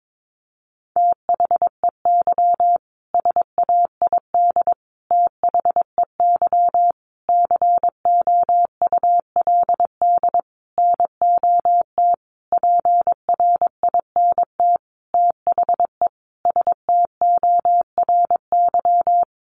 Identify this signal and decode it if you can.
Morse code, heard as a keyed tone, about 22 wpm, 700 Hz. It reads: THEY SAID THEY COULD NOT PRINT THE STORY
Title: Sentences from Top 1400 Words 22wpm